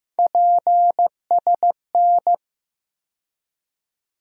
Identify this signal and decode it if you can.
Morse code, heard as a keyed tone, about 15 wpm, 700 Hz. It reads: PSN
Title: USA Call Sign Postfixes - 3 Letters 15wpm